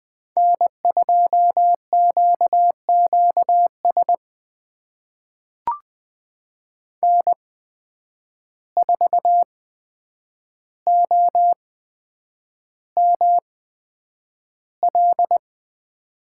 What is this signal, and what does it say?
Morse code, heard as a keyed tone, about 20 wpm, spaced as for 9 wpm, 700 Hz. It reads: N2QQS N4OML
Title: USA Call Signs - 5 Characters - 8x Character Spacing - 20wpm